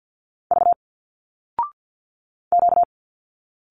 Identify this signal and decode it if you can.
Morse code, heard as a keyed tone, about 48 wpm, 700 Hz. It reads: V Q
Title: Single Letters 50wpm